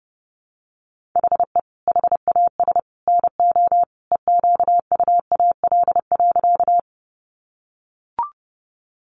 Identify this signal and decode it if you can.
Morse code, heard as a keyed tone, about 30 wpm, 700 Hz. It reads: HE HAS NO EQUAL.
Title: Sentences from Top 1400 Words 30wpm